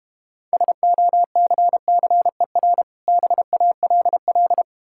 Morse code, heard as a keyed tone, about 32 wpm, 700 Hz. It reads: SOCCER BALL